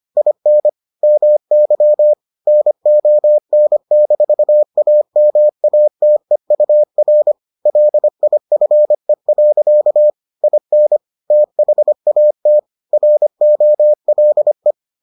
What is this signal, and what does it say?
Morse code, heard as a keyed tone, about 25 wpm, 600 Hz. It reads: IN MY NON-AMATEUR LIFE. IN THAT ROLE